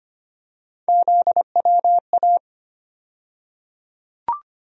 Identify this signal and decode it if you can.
Morse code, heard as a keyed tone, about 25 wpm, 700 Hz. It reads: ZWA
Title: USA Call Sign Postfixes - 3 Letters 25wpm